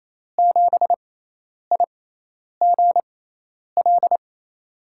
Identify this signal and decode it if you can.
Morse code, heard as a keyed tone, about 28 wpm, spaced as for 10 wpm, 700 Hz. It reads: 7IGL
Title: USA Call Signs - 6 Characters - 6x Character Spacing - 28wpm